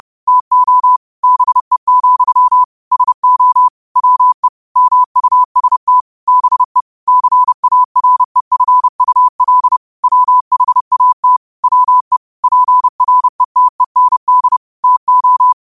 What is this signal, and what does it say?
Morse code, heard as a keyed tone, about 30 wpm, 1000 Hz. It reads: TO BE, SO WE MUST BE CAREFUL WHAT WE PRETEND TO
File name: qotdfiles / 30wpm